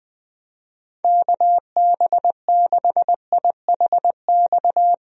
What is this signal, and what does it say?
Morse code, heard as a keyed tone, about 21 wpm, 700 Hz. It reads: KB6IHX
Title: USA Call Signs - 6 Characters - Repeated 3x - 20wpm